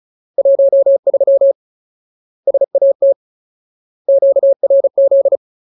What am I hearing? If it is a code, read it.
Morse code, heard as a keyed tone, about 34 wpm, 550 Hz. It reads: 13 SAT QRZ
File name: Ultimate Word List / 35WPM Ultimate